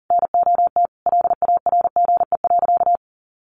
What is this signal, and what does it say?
Morse code, heard as a keyed tone, about 40 wpm, 700 Hz. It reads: NOT LARGE.
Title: Sentences from Top 1200 Words 40wpm